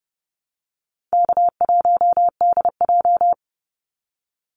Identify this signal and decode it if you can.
Morse code, heard as a keyed tone, about 30 wpm, 700 Hz. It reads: K1DJ
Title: Easy Call Signs Repeated 2x - 30wpm